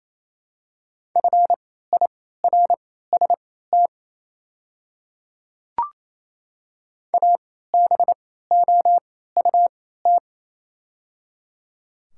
Morse code, heard as a keyed tone, about 28 wpm, spaced as for 16 wpm, 700 Hz. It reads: FIRST ABOUT
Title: 3x Character Spacing - Top 100 Words - 28wpm